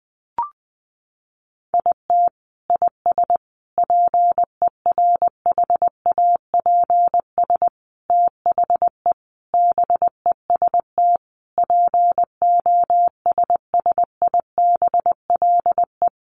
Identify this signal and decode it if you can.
Morse code, heard as a keyed tone, about 20 wpm, 700 Hz. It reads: IT IS PERHAPS THE BEST POSSIBLE
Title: Sentences from Top 1300 Words 20wpm